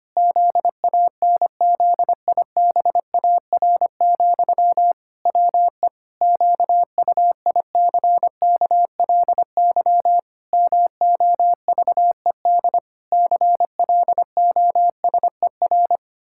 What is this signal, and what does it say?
Morse code, heard as a keyed tone, about 25 wpm, 700 Hz. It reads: ZANZIBAR, WE QUICKLY MOVED CLOSER
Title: Pangram Challenge v2 - 25wpm